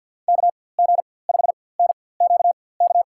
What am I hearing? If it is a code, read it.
Morse code, heard as a keyed tone, about 47 wpm, spaced as for 23 wpm, 700 Hz. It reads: KG5NQK
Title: USA Call Signs - 6 Characters - 4x Character Spacing - 50wpm